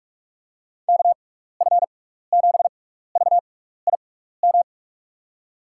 Morse code, heard as a keyed tone, about 43 wpm, spaced as for 15 wpm, 700 Hz. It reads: KF7VIM
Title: USA Call Signs - 6x Character Spacing - 45wpm